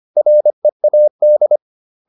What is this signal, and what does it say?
Morse code, heard as a keyed tone, about 25 wpm, 600 Hz. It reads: READ